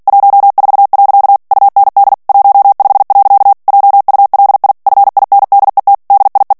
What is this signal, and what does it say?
Morse code, heard as a keyed tone, about 48 wpm, 800 Hz. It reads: 14. UND 15. JULI FINDET DIE
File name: morse mp3 / ALLEMAND texte clair 2 vitesse 45 mots minute